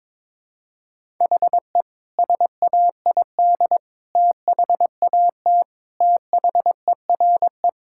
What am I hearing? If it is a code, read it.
Morse code, heard as a keyed tone, about 22 wpm, 700 Hz. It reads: HE SAID THAT THERE
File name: practice / 22wpm / Sentences with 4-Word Phrases - 22wpm